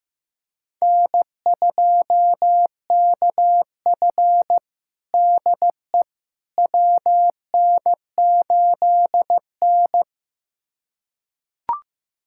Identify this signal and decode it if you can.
Morse code, heard as a keyed tone, about 15 wpm, 700 Hz. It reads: N2KF DE WN8N
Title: USA Callsign DE Callsign - 4 Characters - 15wpm